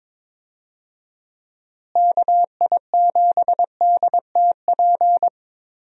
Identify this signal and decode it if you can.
Morse code, heard as a keyed tone, about 22 wpm, 700 Hz. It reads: KI7DTP
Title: USA Call Signs - 6 Characters - Repeated 3x - 22wpm